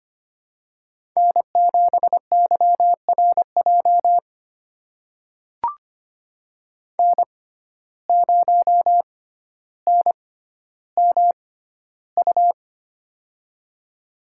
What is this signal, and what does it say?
Morse code, heard as a keyed tone, about 25 wpm, spaced as for 13 wpm, 700 Hz. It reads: N7YRJ N0NMU
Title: USA Call Signs - 5 Characters - 6x Character Spacing - 25wpm